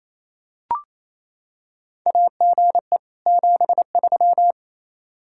Morse code, heard as a keyed tone, about 28 wpm, 700 Hz. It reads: AGE 73
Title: Practice QSO - Exchange 3 - More Info - 28wpm